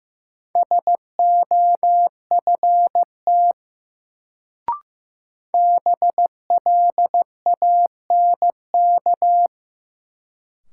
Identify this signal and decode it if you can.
Morse code, heard as a keyed tone, about 15 wpm, 700 Hz. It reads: SOFT BLANK
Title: Sentences from Top 1500 Words - New Word Drill - 15wpm